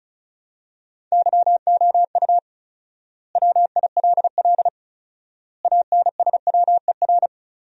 Morse code, heard as a keyed tone, about 34 wpm, 700 Hz. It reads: YOU WILL ANSWER
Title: 4x Word-Spacing - Sentences from Top 300 Words - 35wpm